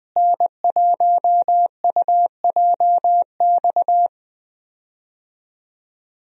Morse code, heard as a keyed tone, about 20 wpm, 700 Hz. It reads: N1UJX
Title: USA Call Signs - 5 Characters - Repeated 2x - 20wpm